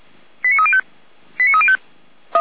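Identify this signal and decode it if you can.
Five-tone selective call (ZVEI), heard as quick runs of five tones, tones of about 70 ms. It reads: 89305 89305
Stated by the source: ZVEI five-tone selective call received on an SDR.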